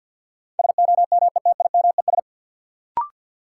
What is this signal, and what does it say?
Morse code, heard as a keyed tone, about 47 wpm, 700 Hz. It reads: SOMETIMES
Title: Sentences from Top 600 Words - New Word Drill - 50wpm